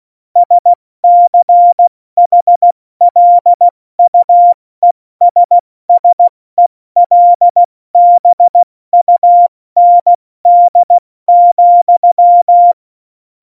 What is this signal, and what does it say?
Morse code, heard as a keyed tone, about 16 wpm, spaced as for 14 wpm, 700 Hz. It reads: SCHLUESSELBUND,
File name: Morsekurs / Witze / Witz 17.Juli 2018, 70 F80